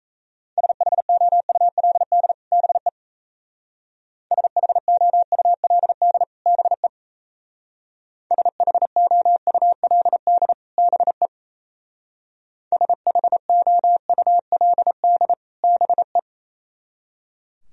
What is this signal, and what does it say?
Morse code, heard as a keyed tone, about 37 wpm, 700 Hz. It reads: SHOULD BE SHOULD BE SHOULD BE SHOULD BE